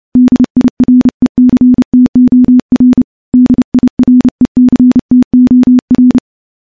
Morse code, heard as a keyed tone, about 30 wpm, 250 Hz. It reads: DIRECTOR DIRECTOR
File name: morse / morse-training-30WPM